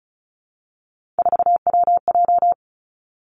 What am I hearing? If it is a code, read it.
Morse code, heard as a keyed tone, about 36 wpm, 700 Hz. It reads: 4WJ
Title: Three Challenging Letter-Number Combinations 35wpm